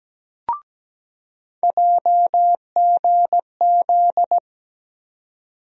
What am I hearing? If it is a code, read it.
Morse code, heard as a keyed tone, about 17 wpm, 700 Hz. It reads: JGZ
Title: USA Call Sign Postfixes - 3 Letters 17wpm